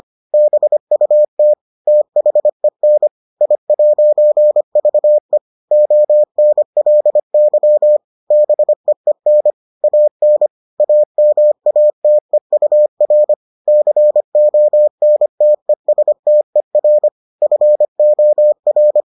Morse code, heard as a keyed tone, about 25 wpm, 600 Hz. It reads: BUT THEN I'VE ONLY BEEN AN AMATEUR CONTESTER FOR